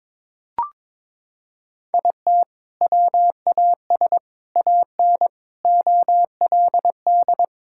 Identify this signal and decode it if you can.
Morse code, heard as a keyed tone, about 22 wpm, 700 Hz. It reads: IT WAS AN OLD
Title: Sentences from Top 1000 Words - Review Part 5 - 22wpm